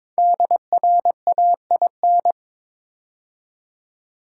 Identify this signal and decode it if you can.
Morse code, heard as a keyed tone, about 22 wpm, 700 Hz. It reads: DRAIN
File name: 5 Letter Words - First 3 Letters Given - 22wpm